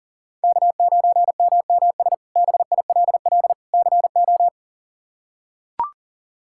Morse code, heard as a keyed tone, about 40 wpm, 700 Hz. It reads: K9MMS BILL CO
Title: K1USN SST QSO Practice - 40wpm